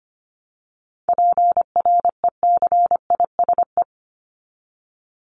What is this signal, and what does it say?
Morse code, heard as a keyed tone, about 25 wpm, 700 Hz. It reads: PRECISE
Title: LY Suffix - Top 100 Words with Base Words 25wpm